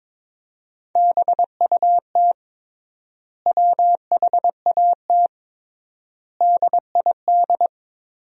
Morse code, heard as a keyed tone, about 22 wpm, 700 Hz. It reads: BUT WHAT DID
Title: 3x Word-Spacing - Sentences from Top 300 Words - 22wpm